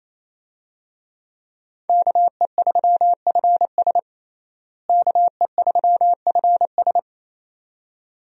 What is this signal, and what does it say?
Morse code, heard as a keyed tone, about 28 wpm, 700 Hz. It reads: KE3FS KE3FS
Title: USA Call Signs - 5 Characters - Repeated 2x - 28wpm